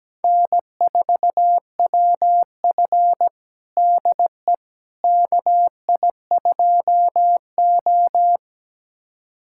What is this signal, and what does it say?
Morse code, heard as a keyed tone, about 17 wpm, 700 Hz. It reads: N4WF DE KI2O
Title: USA Callsign DE Callsign - 4 Characters - 17wpm